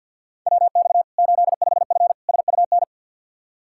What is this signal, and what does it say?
Morse code, heard as a keyed tone, about 52 wpm, 700 Hz. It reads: WX 85F SUN
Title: Practice QSO - Exchange 2 - Info - 50wpm